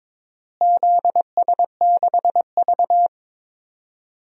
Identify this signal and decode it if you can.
Morse code, heard as a keyed tone, about 21 wpm, 700 Hz. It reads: ZS6V
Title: DXCC-100 Call Signs - 4 Characters - Repeated 2x - 22wpm